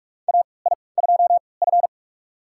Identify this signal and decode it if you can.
Morse code, heard as a keyed tone, about 42 wpm, spaced as for 25 wpm, 700 Hz. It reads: AI2F
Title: USA Call Signs - 4 Characters - 3x Character Spacing - 45wpm